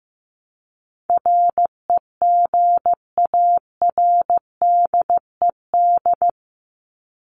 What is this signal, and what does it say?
Morse code, heard as a keyed tone, about 15 wpm, 700 Hz. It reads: REGARDED
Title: ED Suffix - Top 100 Words with Base Words 15wpm